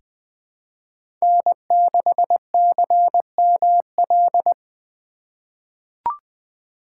Morse code, heard as a keyed tone, about 20 wpm, 700 Hz. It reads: N6CML
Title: USA Call Signs - 4x Character Spacing - 20wpm